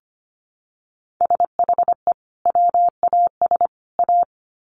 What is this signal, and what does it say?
Morse code, heard as a keyed tone, about 25 wpm, 700 Hz. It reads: SHE WAS A
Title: Sets of 3 Words 25wpm